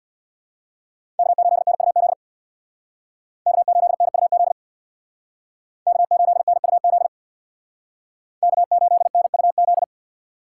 Morse code, heard as a keyed tone, about 63 wpm, 700 Hz. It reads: K8NUZ K8NUZ K8NUZ K8NUZ